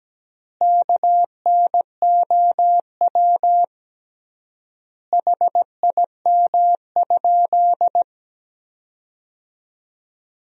Morse code, heard as a keyed tone, about 17 wpm, 700 Hz. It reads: KNOW HIM?
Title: 3x Word-Spacing - Sentences from Top 100 Words - 17wpm